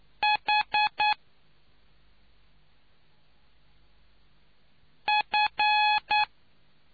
Morse code, heard as a keyed tone, about 9 wpm, 840 Hz. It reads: H F